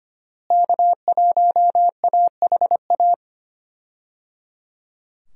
Morse code, heard as a keyed tone, about 25 wpm, 700 Hz. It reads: K1AHA
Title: USA Call Signs - 5 Characters - Repeated 3x - 25wpm